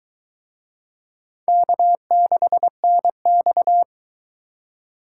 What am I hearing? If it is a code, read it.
Morse code, heard as a keyed tone, about 22 wpm, 700 Hz. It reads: K6NX